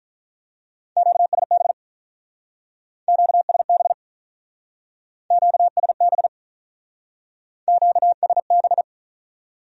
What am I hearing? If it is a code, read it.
Morse code, heard as a keyed tone, about 45 wpm, 700 Hz. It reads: QSB QSB QSB QSB